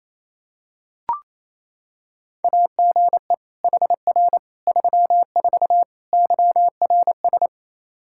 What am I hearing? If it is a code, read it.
Morse code, heard as a keyed tone, about 28 wpm, 700 Hz. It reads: AGE HR 34 YRS
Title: Practice QSO - Exchange 3 - More Info - 28wpm